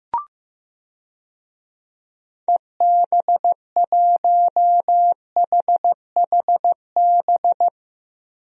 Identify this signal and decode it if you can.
Morse code, heard as a keyed tone, about 15 wpm, 700 Hz. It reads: EB1HHB
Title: DXCC-100 Call Signs - 6 Characters - Repeated 2x - 15wpm